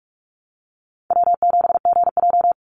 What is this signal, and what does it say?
Morse code, heard as a keyed tone, about 44 wpm, 700 Hz. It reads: W7GJ
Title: NA SOTA Chaser Callsigns 45wpm